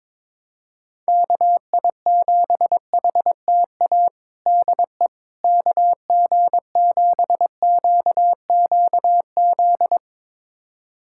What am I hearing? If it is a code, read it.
Morse code, heard as a keyed tone, about 21 wpm, 700 Hz. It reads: KI7HTA DE KG7QQZ
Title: USA Callsign DE Callsign - 6 Characters - 22wpm